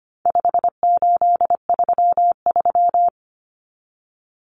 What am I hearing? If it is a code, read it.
Morse code, heard as a keyed tone, about 25 wpm, 700 Hz. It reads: 5833
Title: Numbers - 4-Digit 25wpm